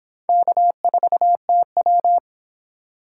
Morse code, heard as a keyed tone, about 23 wpm, 700 Hz. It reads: K4TW